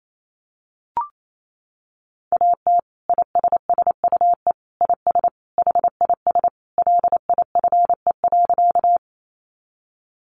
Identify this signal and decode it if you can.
Morse code, heard as a keyed tone, about 28 wpm, 700 Hz. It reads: AT ISSUE IS HIS LIFE.
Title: Sentences from Top 900 Words 28wpm